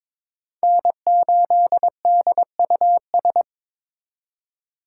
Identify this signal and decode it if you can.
Morse code, heard as a keyed tone, about 22 wpm, 700 Hz. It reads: N8DUS
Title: K1USN SST QSO Practice - 22wpm